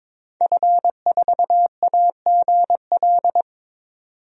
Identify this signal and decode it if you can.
Morse code, heard as a keyed tone, about 22 wpm, 700 Hz. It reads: F4AGL
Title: DXCC-100 Call Signs - 5 Characters - 22wpm